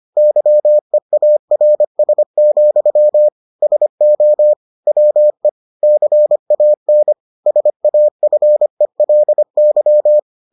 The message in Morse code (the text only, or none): YEARS, SO WE CAN SAFELY